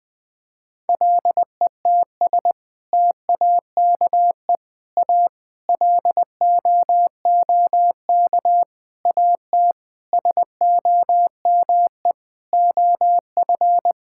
Morse code, heard as a keyed tone, about 20 wpm, 700 Hz. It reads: LETS TAKE A LOOK AT SOME OF
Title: Sets of 8 Words - 20wpm